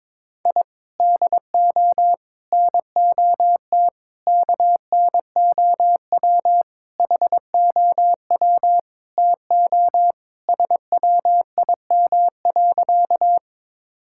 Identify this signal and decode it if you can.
Morse code, heard as a keyed tone, about 22 wpm, 700 Hz. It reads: I DO NOT KNOW HOW TO SWIM.
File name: practice / 22wpm / Sentences with 4-Word Phrases - 22wpm